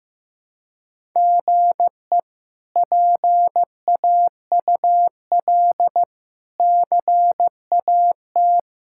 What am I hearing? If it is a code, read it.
Morse code, heard as a keyed tone, about 15 wpm, 700 Hz. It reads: GE PAUL CAT
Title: K1USN SST QSO Practice - 15wpm